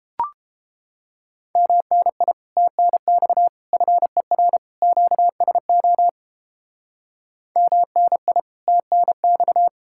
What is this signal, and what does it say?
Morse code, heard as a keyed tone, about 33 wpm, 700 Hz. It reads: MNI TNX FER QSO MNI TNX